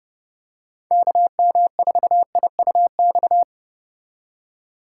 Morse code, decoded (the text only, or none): KM4IUX